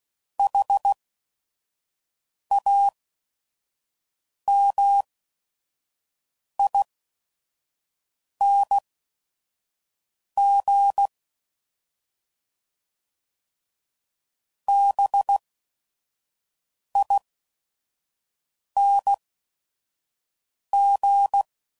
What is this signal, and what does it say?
Morse code, heard as a keyed tone, about 16 wpm, spaced as for 5 wpm, 800 Hz. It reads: HAMING BING
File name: Suffixes / Special ING's / VerySlowToFaster ING's